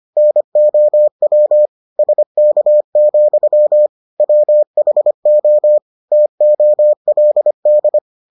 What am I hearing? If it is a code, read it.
Morse code, heard as a keyed tone, about 25 wpm, 600 Hz. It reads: NOW SK, WHO TOLD